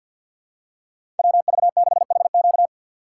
Morse code, heard as a keyed tone, about 54 wpm, 700 Hz. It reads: WV6HQ
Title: USA Call Signs - 5 Characters - 8x Character Spacing - 50wpm